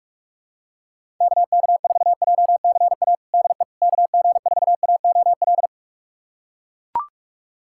Morse code, heard as a keyed tone, about 41 wpm, 700 Hz. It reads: KK4JCA DE KG4AOL
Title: USA Callsign DE Callsign - 6 Characters - 45wpm